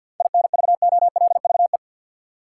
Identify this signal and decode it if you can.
Morse code, heard as a keyed tone, about 53 wpm, 700 Hz. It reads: INVOLVE
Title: Seven Letter Words - 50wpm